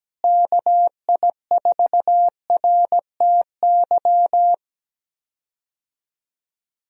Morse code, heard as a keyed tone, about 17 wpm, 700 Hz. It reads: KI4RTY
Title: USA Call Signs - 6 Characters - Repeated 2x - 17wpm